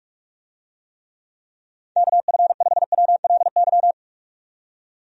Morse code, heard as a keyed tone, about 46 wpm, 700 Hz. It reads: KF5WLY
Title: USA Call Signs - 6 Characters - 45wpm